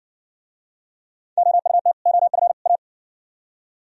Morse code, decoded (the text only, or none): OUT OF A